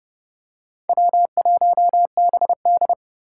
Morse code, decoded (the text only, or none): W1BD